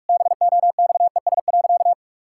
Morse code, heard as a keyed tone, about 43 wpm, 700 Hz. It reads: BOXES.
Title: Pangram Challenge v2 - 45wpm